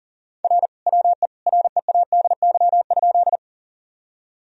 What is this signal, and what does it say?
Morse code, heard as a keyed tone, about 40 wpm, 700 Hz. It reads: R WE READY?
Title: Sentences from Top 400 Words and Abbreviations 40wpm